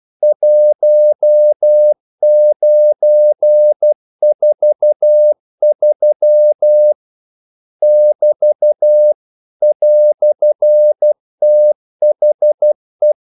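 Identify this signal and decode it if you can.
Morse code, heard as a keyed tone, about 12 wpm, 600 Hz. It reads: 1943 <BT> "THE
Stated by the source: Morse practice sending English quotations.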